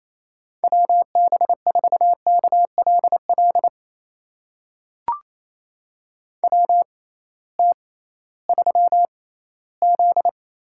Morse code, heard as a keyed tone, about 27 wpm, spaced as for 16 wpm, 700 Hz. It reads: WB4KLL WT3Z
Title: USA Call Signs - 6x Character Spacing - 28wpm